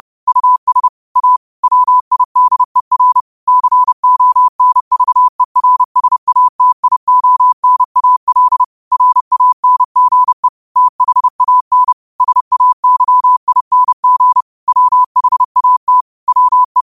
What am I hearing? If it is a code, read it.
Morse code, heard as a keyed tone, about 30 wpm, 1000 Hz. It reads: US A WIDER CONVERSATIONAL RANGE THAN SAYING WHAT WE
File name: qotdfiles / 30wpm